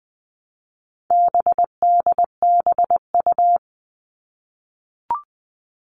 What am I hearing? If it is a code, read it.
Morse code, heard as a keyed tone, about 20 wpm, 700 Hz. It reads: BDBU
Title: All Cut Numbers - 4-Digit 20wpm